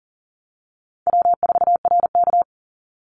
Morse code, heard as a keyed tone, about 40 wpm, 700 Hz. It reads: W4RK
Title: NA SOTA Chaser Callsigns 40wpm